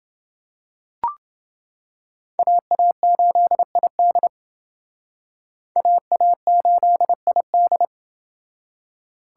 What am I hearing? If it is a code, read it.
Morse code, heard as a keyed tone, about 28 wpm, 700 Hz. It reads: AA8ID AA8ID